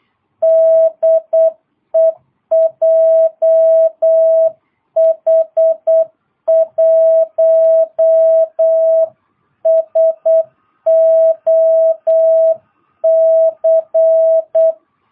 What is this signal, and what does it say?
Morse code, heard as a keyed tone, about 8 wpm, 660 Hz. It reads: DE JH1SOC
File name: jh8cbh / 2022cw